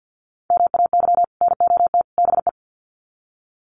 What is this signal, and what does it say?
Morse code, heard as a keyed tone, about 50 wpm, 700 Hz. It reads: MAY NOT BE
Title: Sets of 3 Words 50wpm